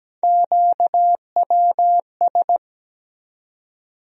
Morse code, heard as a keyed tone, about 17 wpm, 700 Hz. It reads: QWS
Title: USA Call Sign Postfixes - 3 Letters 17wpm